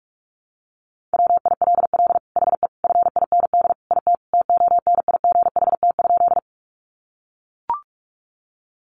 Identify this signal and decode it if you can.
Morse code, heard as a keyed tone, about 44 wpm, 700 Hz. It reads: WILL HE FIND IT TONIGHT?
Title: Sentences from Top 600 Words 45wpm